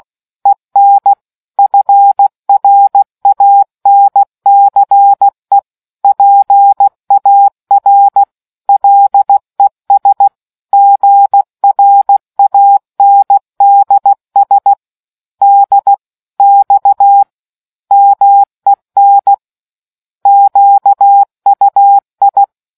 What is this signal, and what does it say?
Morse code, heard as a keyed tone, about 16 wpm, 800 Hz. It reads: EN FRANCE PAR LES GRANDS D X MEN QUI